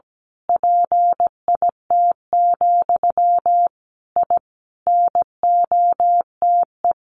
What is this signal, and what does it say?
Morse code, heard as a keyed tone, about 17 wpm, 700 Hz. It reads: PIT, I NOTE
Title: A Sound Of Thunder 17wpm